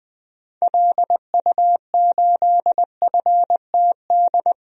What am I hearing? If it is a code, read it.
Morse code, heard as a keyed tone, about 20 wpm, 700 Hz. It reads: LU8FTD
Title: DXCC-100 Call Signs - 6 Characters - Repeated 2x - 20wpm